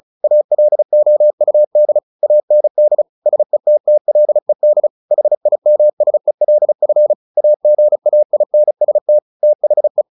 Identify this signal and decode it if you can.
Morse code, heard as a keyed tone, about 35 wpm, 600 Hz. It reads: ALOUD AND SETTLED HIMSELF AGAINST THE
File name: the old man and the sea / the old man and the sea 1 35wpm